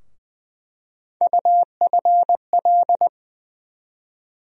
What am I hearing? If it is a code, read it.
Morse code, heard as a keyed tone, about 20 wpm, 700 Hz. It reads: UFL
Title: USA Call Sign Postfixes - 3 Letters 20wpm